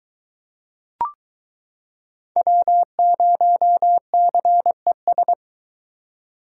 Morse code, heard as a keyed tone, about 22 wpm, 700 Hz. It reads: W0CES